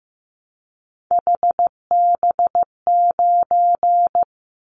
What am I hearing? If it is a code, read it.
Morse code, heard as a keyed tone, about 15 wpm, 700 Hz. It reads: HB9
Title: Callsign Prefixes for Top 25 DXCC - 15wpm